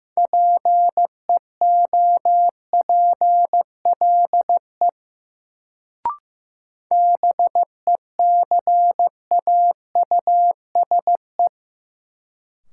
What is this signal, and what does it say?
Morse code, heard as a keyed tone, about 15 wpm, 700 Hz. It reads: PEOPLE BECAUSE
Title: Sets of 2 Words - New Word Drill - 15wpm